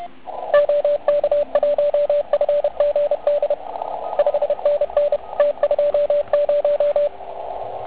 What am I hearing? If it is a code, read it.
Morse code, heard as a keyed tone, about 32 wpm, 580 Hz, with radio noise. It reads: OK1FGD 5NN T20
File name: Archiv / CUC138 / OK1IF CUC138